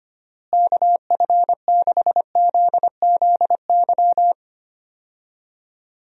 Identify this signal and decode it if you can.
Morse code, heard as a keyed tone, about 25 wpm, 700 Hz. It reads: KF6ZZY
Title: USA Call Signs - 6 Characters - Repeated 3x - 25wpm